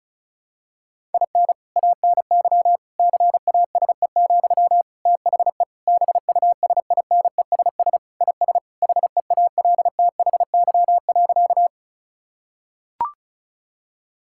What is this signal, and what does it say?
Morse code, heard as a keyed tone, about 36 wpm, 700 Hz. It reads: IN ANY CASE, THE BUSINESS IS HEALTHY.
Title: Sentences from Top 1000 Words - Review Part 5 - 35wpm